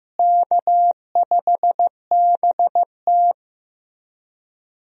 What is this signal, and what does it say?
Morse code, heard as a keyed tone, about 15 wpm, 700 Hz. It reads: K5BT
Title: USA Call Signs - 4 Characters - Repeated 3x - 15wpm